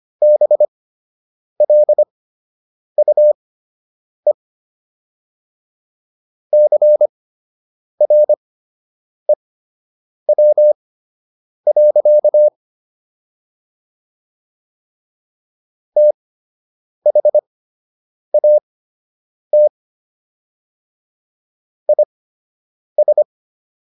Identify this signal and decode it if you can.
Morse code, heard as a keyed tone, about 25 wpm, spaced as for 8 wpm, 600 Hz. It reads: BLUE CREW. THAT IS